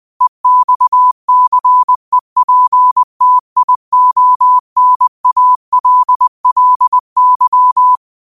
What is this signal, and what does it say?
Morse code, heard as a keyed tone, about 20 wpm, 1000 Hz. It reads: EXCEPTIONALLY
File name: qotdfiles / 20wpm